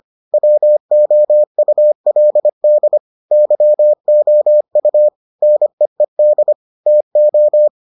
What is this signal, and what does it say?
Morse code, heard as a keyed tone, about 25 wpm, 600 Hz. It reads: WOULD YOU NEED TO